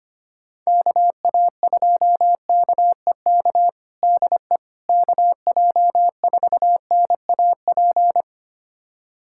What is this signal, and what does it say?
Morse code, heard as a keyed tone, about 25 wpm, 700 Hz. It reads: KA2KEK DE KJ4NAP
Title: USA Callsign DE Callsign - 6 Characters - 25wpm